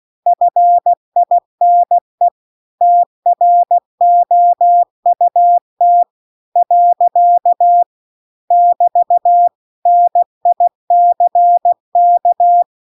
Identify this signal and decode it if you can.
Morse code, heard as a keyed tone, about 16 wpm, 700 Hz. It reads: FINE TROUT . <BT> NICK